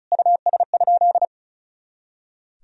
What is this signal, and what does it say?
Morse code, heard as a keyed tone, about 36 wpm, 700 Hz. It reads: US?
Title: Sentences from Top 200 Words and Abbreviations 35wpm